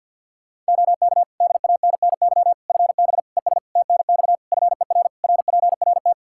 Morse code, heard as a keyed tone, about 50 wpm, 700 Hz. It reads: OK DANNY FB ES TNX FER RPRT
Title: Practice QSO - Exchange 2 - Info - 50wpm